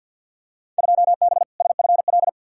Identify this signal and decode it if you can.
Morse code, heard as a keyed tone, about 52 wpm, 700 Hz. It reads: 2B SFL
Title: Field Day Exchanges - 50wpm